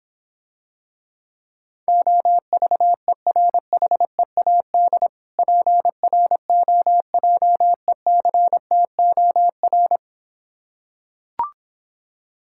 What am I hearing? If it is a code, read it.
Morse code, heard as a keyed tone, about 26 wpm, 700 Hz. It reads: OVERHEAD PROJECTOR